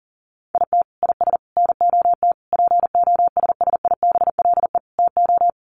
Morse code, heard as a keyed tone, about 40 wpm, 700 Hz. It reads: IT IS NOT POSSIBLE TO
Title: Sets of 5 Words Encore2 40wpm